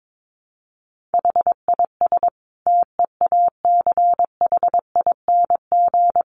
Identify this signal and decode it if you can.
Morse code, heard as a keyed tone, about 22 wpm, 700 Hz. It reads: HIS TEACHING
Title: Sentences from Top 1100 Words 22wpm